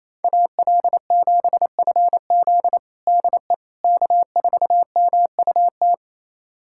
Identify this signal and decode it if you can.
Morse code, heard as a keyed tone, about 28 wpm, 700 Hz. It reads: AL7FZ DE K4MUT
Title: USA Callsign DE Callsign - 5 Characters - 28wpm